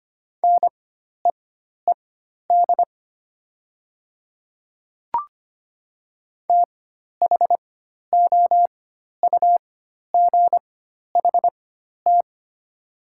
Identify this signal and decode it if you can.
Morse code, heard as a keyed tone, about 25 wpm, spaced as for 12 wpm, 700 Hz. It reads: NEED THOUGHT
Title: 4x Character Spacing - Top 300 Words - 25wpm